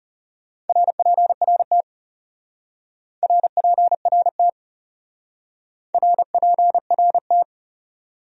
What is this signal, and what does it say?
Morse code, heard as a keyed tone, about 39 wpm, 700 Hz. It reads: RPRT RPRT RPRT